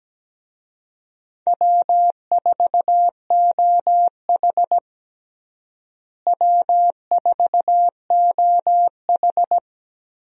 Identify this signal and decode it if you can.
Morse code, heard as a keyed tone, about 17 wpm, 700 Hz. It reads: W4OH W4OH
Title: USA Call Signs - 4 Characters - Repeated 3x - 17wpm